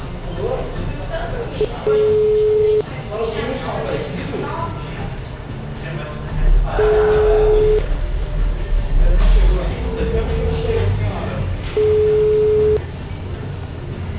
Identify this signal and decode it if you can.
Ringback tone (Europe, 425 Hz), with speech.